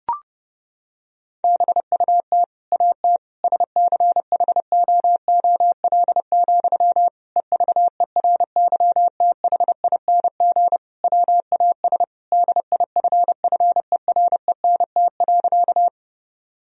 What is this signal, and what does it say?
Morse code, heard as a keyed tone, about 30 wpm, 700 Hz. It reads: BUT AT SCHOOL, EVERYTHING WAS DIFFERENT.
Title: Sentences from Top 1000 Words - Review Part 3 - 30wpm